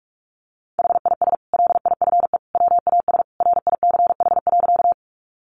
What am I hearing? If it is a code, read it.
Morse code, heard as a keyed tone, about 44 wpm, 700 Hz. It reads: HIS LIFE WAS RICH.
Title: Sentences from Top 1100 Words 45wpm